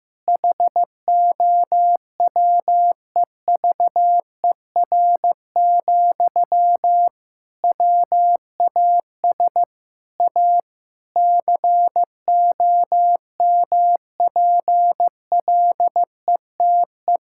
Morse code, heard as a keyed tone, about 15 wpm, 700 Hz. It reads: HOWEVER, WAS A COMPLETE
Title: Sentences from Top 1000 Words - Review Part 6 - 15wpm